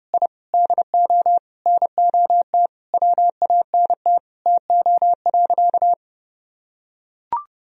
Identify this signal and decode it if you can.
Morse code, heard as a keyed tone, about 28 wpm, 700 Hz. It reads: I DO NOT WANT TO.
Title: Sentences from Top 1000 Words - Review Part 3 - 30wpm